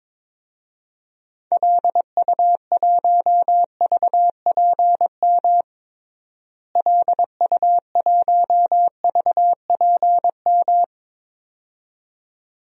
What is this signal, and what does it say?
Morse code, heard as a keyed tone, about 22 wpm, 700 Hz. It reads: LU1VPM LU1VPM
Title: DXCC-100 Call Signs - 6 Characters - Repeated 2x - 22wpm